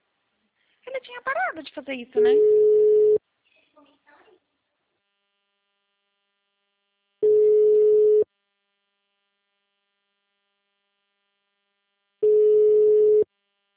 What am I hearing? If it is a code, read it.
Ringback tone (Europe, 425 Hz), with speech.